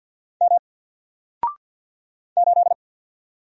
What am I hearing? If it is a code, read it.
Morse code, heard as a keyed tone, about 48 wpm, 700 Hz. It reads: M 8
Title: Single Character (Letter:Number:Punctuation:Prosign) Alt 50wpm